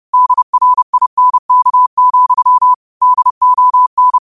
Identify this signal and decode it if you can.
Morse code, heard as a keyed tone, about 30 wpm, 1000 Hz. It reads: DRINK, DON